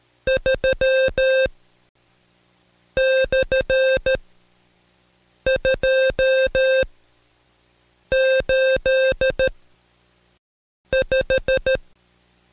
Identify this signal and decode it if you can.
Morse code, heard as a keyed tone, about 13 wpm, spaced as for 5 wpm, 530 Hz. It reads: 3/285
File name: Code Quick / AudioDownld2 / Seg107-Day13-DayDrill 13